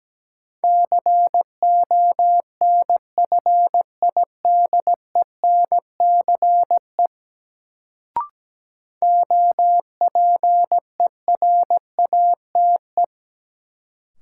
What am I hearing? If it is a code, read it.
Morse code, heard as a keyed tone, about 17 wpm, 700 Hz. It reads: CONFIDENCE OPERATE
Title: Sentences from Top 1500 Words - New Word Drill - 17wpm